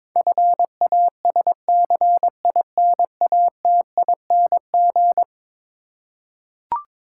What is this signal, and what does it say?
Morse code, heard as a keyed tone, about 22 wpm, 700 Hz. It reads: FASCINATING
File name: practice / 22wpm / Eleven Letter Words - 22wpm